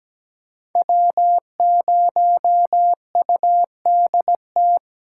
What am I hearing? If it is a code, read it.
Morse code, heard as a keyed tone, about 20 wpm, 700 Hz. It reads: W0UDT
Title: Speed Racing - USA 1x3 Call Signs 17wpm